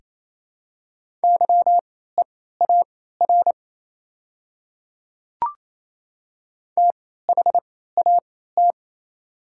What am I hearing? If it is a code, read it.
Morse code, heard as a keyed tone, about 28 wpm, spaced as for 16 wpm, 700 Hz. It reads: YEAR THAT
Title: 3x Character Spacing - Top 200 Words - 28wpm